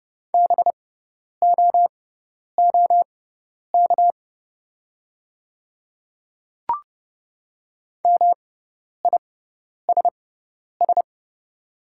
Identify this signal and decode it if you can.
Morse code, heard as a keyed tone, about 29 wpm, spaced as for 10 wpm, 700 Hz. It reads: BOOK MISS
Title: 6x Character Spacing - Top 500 Words - 30wpm